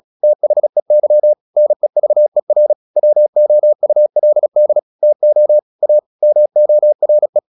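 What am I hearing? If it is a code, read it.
Morse code, heard as a keyed tone, about 36 wpm, 600 Hz. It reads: THEY NEVER WOULD TO A MORE
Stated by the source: Morse practice at 36 wpm, text of Ken Follett's novel Never, from CW med Gnister.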